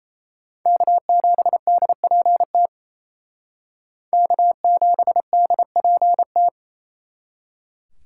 Morse code, heard as a keyed tone, about 35 wpm, 700 Hz. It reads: K7DPT K7DPT